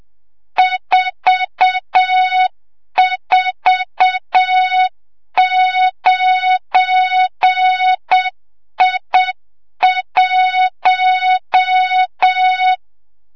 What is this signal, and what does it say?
Morse code, heard as a keyed tone, about 7 wpm, 3700 Hz, with noise. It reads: 449I1
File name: Morze-HA2MN / hang / 25 GYAKORLAT